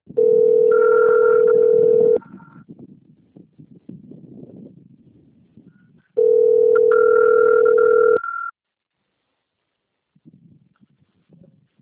Ringback tone (North America), 440 + 480 Hz.
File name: out- - - - - .93986